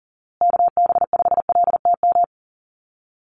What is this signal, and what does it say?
Morse code, heard as a keyed tone, about 40 wpm, 700 Hz. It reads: KB5LTM
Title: USA Call Signs Repeated 2x 40wpm